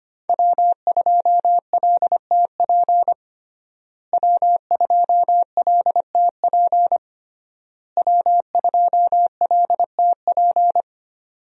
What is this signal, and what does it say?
Morse code, heard as a keyed tone, about 25 wpm, 700 Hz. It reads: W2LTP W2LTP W2LTP
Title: USA Call Signs - 5 Characters - Repeated 3x - 25wpm